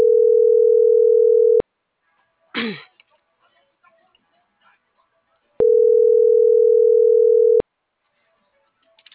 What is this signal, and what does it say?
Ringback tone (North America), 440 + 480 Hz.